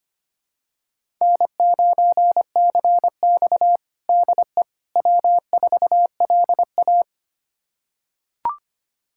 Morse code, heard as a keyed tone, about 25 wpm, 700 Hz. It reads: N9CX DE W4LA
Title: USA Callsign DE Callsign - 4 Characters - 25wpm